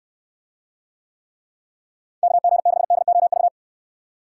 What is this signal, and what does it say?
Morse code, heard as a keyed tone, about 64 wpm, 700 Hz. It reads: KG6DOV